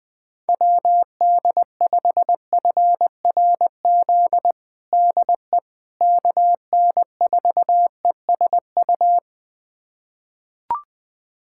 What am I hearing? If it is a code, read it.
Morse code, heard as a keyed tone, about 20 wpm, 700 Hz. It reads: WD5FRZ DE KN4ESU
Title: USA Callsign DE Callsign 20wpm